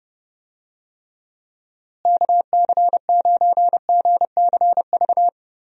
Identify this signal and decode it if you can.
Morse code, heard as a keyed tone, about 30 wpm, 700 Hz. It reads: KC9GCV
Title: USA Call Signs - 6 Characters - Repeated 2x - 30wpm